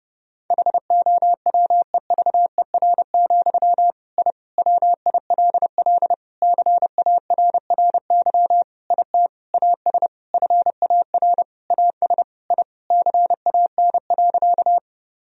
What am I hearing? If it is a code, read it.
Morse code, heard as a keyed tone, about 30 wpm, 700 Hz. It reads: HOWEVER, I WILL CARRY IT AS FAR AS I CAN.
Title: Sentences from Top 1000 Words - Review Part 1 - 30wpm